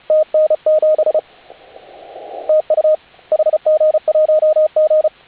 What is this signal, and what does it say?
Morse code, heard as a keyed tone, about 34 wpm, 620 Hz, with radio noise. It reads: TN7 TU HG1G
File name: HA6A CW